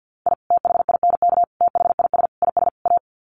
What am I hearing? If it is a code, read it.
Morse code, heard as a keyed tone, about 50 wpm, 700 Hz. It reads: I THINK THIS IS A
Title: Sets of 5 Words 50wpm